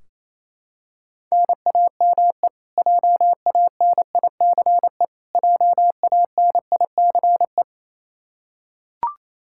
Morse code, heard as a keyed tone, about 28 wpm, 700 Hz. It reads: NAME JANICE JANICE
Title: Practice QSO - Exchange 1 - Beginning - 28wpm